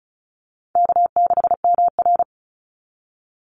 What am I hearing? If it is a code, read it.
Morse code, heard as a keyed tone, about 34 wpm, 700 Hz. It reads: K6MR
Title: Easy Call Signs Repeated 3x - 35wpm